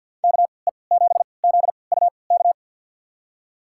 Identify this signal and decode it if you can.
Morse code, heard as a keyed tone, about 52 wpm, spaced as for 29 wpm, 690 Hz. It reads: KE7ZUK
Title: USA Call Signs - 3x Character Spacing - 50wpm